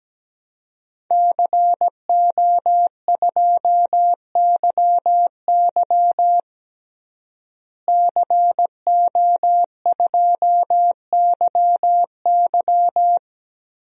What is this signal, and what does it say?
Morse code, heard as a keyed tone, about 17 wpm, 700 Hz. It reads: CO2YY CO2YY
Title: DXCC-100 Call Signs - 5 Characters - Repeated 3x - 17wpm